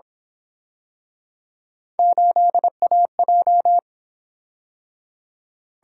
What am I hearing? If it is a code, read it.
Morse code, heard as a keyed tone, about 26 wpm, 700 Hz. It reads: 8AJ
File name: Speed-Racing - Three Letter-Number Combinations - 22wpm